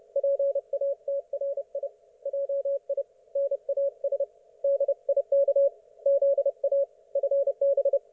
Morse code, heard as a keyed tone, about 30 wpm, 560 Hz. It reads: PATRI JI NAS DIK ZA FB
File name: Archiv / CUC179 / ok1fhi cul179